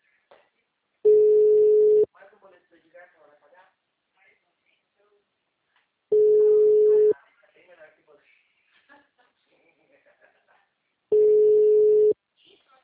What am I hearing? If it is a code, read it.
Ringback tone (Europe, 425 Hz), with speech.